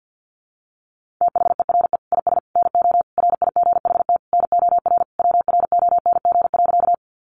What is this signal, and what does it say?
Morse code, heard as a keyed tone, about 50 wpm, 700 Hz. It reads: THERE IS NO RIGHT NOR WRONG.
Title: Sentences from Top 1500 Words 50wpm